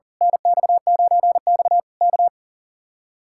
Morse code, heard as a keyed tone, about 37 wpm, 700 Hz. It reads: NX9X K
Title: Speed-Racing Common QSO Phrases - 40wpm